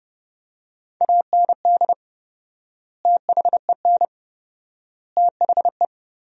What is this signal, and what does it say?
Morse code, heard as a keyed tone, about 30 wpm, 700 Hz. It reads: AND THEN THE
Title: 4x Word-Spacing - Sentences from Top 500 Words - 30wpm